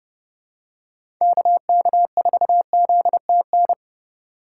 Morse code, heard as a keyed tone, about 30 wpm, 700 Hz. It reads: KK4ZTN
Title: USA Call Signs - 6 Characters - 3x Character Spacing - 30wpm